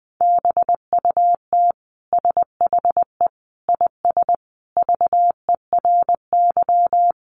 Morse code, heard as a keyed tone, about 20 wpm, 700 Hz. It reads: BUT SHE IS VERY
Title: Sentences from Top 400 Words 20wpm